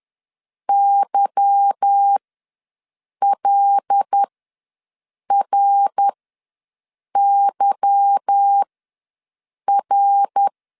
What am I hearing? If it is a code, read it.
Morse code, heard as a keyed tone, about 10 wpm, spaced as for 6 wpm, 800 Hz. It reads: YLRYR
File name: LEZIONI / 8 YJ45 / 8 Y ws 10wpm 3952